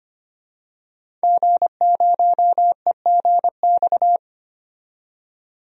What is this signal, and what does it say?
Morse code, heard as a keyed tone, about 25 wpm, 700 Hz. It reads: G0EGX
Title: DXCC-100 Call Signs - 5 Characters - 25wpm